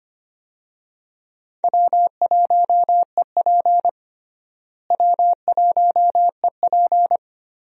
Morse code, heard as a keyed tone, about 25 wpm, 700 Hz. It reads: W1EP W1EP
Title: USA Call Signs - 4 Characters - Repeated 3x - 25wpm